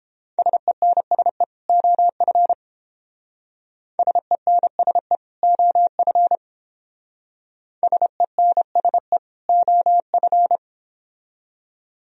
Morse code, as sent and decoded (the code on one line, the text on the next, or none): ... -. ... / --- ..-. / ... -. ... / --- ..-. / ... -. ... / --- ..-.
SENSE OF SENSE OF SENSE OF